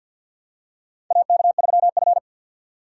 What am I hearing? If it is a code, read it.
Morse code, heard as a keyed tone, about 53 wpm, 700 Hz. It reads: AK3F
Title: USA Call Signs - 6x Character Spacing - 50wpm